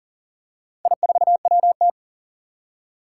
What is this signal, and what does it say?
Morse code, heard as a keyed tone, about 40 wpm, 700 Hz. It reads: I4WT
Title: Four Challenging Letter-Number Combinations 40wpm